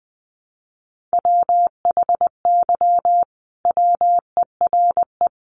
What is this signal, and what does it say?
Morse code, heard as a keyed tone, about 20 wpm, 700 Hz. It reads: WHY WERE
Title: Sentences from Top 1400 Words 20wpm